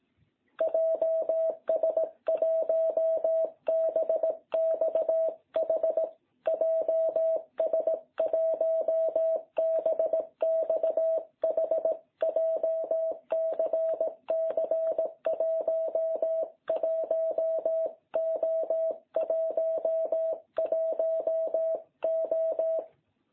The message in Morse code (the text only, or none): JS1BXH JS1BXH JCC 11O11O